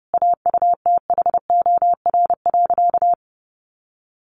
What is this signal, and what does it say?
Morse code, heard as a keyed tone, about 30 wpm, 700 Hz. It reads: AUTHOR.
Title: Sentences from Top 1200 Words 30wpm